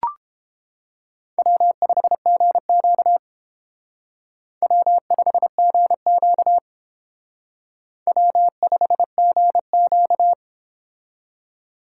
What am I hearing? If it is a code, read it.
Morse code, heard as a keyed tone, about 31 wpm, 700 Hz. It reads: W5GQ W5GQ W5GQ